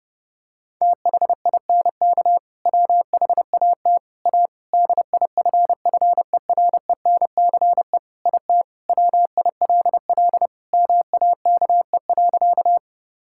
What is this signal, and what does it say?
Morse code, heard as a keyed tone, about 32 wpm, 700 Hz. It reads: THINK WHAT A DIFFERENCE IT WILL MAKE.
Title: Sentences from Top 1000 Words - Review Part 5 - 30wpm